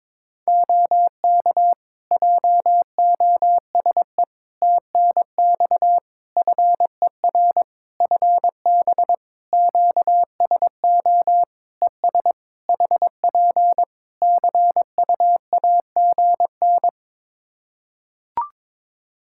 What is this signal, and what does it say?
Morse code, heard as a keyed tone, about 22 wpm, 700 Hz. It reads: OK JOSE TNX FER FB QSO ES HP CUAGN
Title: K1USN SST QSO Practice 22wpm